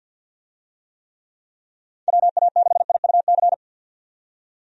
Morse code, heard as a keyed tone, about 49 wpm, 700 Hz. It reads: WA6IUC
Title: USA Call Signs - 6 Characters - 50wpm